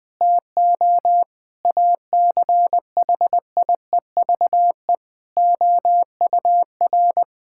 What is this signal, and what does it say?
Morse code, heard as a keyed tone, about 21 wpm, 700 Hz. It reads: TO ACHIEVE OUR
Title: Sentences from Top 1000 Words - Review Part 6 - 20wpm